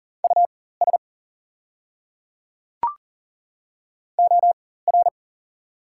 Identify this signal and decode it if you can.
Morse code, heard as a keyed tone, about 40 wpm, spaced as for 19 wpm, 690 Hz. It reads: US OR
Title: 4x Character Spacing - Top 200 Words - 40wpm